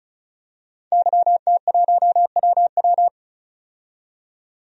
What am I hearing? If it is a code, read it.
Morse code, heard as a keyed tone, about 34 wpm, 700 Hz. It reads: YT1WW
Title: DXCC-100 Call Signs - 5 Characters - 35wpm